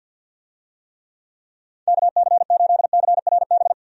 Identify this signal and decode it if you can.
Morse code, heard as a keyed tone, about 50 wpm, 700 Hz. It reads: KC8CRB
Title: USA Call Signs - 6 Characters - 50wpm